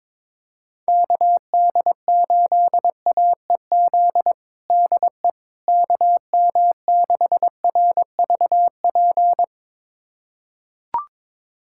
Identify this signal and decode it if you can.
Morse code, heard as a keyed tone, about 22 wpm, 700 Hz. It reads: KD8AEZ DE KM6RVP
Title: USA Callsign DE Callsign - 6 Characters - 22wpm